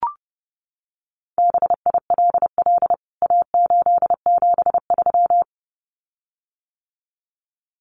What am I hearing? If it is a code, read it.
Morse code, heard as a keyed tone, about 30 wpm, 700 Hz. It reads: BILL A873
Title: CWT Exchange 30wpm